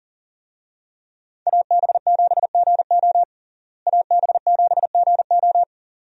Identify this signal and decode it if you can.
Morse code, heard as a keyed tone, about 40 wpm, 700 Hz. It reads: AB7GO AB7GO
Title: USA Call Signs - 5 Characters - Repeated 2x - 40wpm